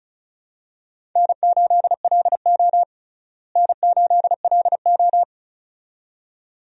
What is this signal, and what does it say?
Morse code, heard as a keyed tone, about 36 wpm, 700 Hz. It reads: N8LO N8LO
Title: USA Call Signs - 4 Characters - Repeated 2x - 35wpm